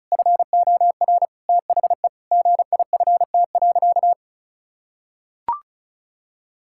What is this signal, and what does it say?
Morse code, heard as a keyed tone, about 36 wpm, 700 Hz. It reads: FOR THE GIFT.
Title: Sentences from Top 1000 Words - Review Part 6 - 35wpm